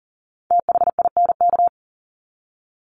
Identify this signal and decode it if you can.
Morse code, heard as a keyed tone, about 40 wpm, 700 Hz. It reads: THINK
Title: Top 200 Words 40wpm